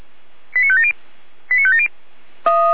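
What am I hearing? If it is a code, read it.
Five-tone selective call (ZVEI), heard as quick runs of five tones, tones of about 70 ms. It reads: 89570 89570
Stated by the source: ZVEI five-tone selective call received on an SDR.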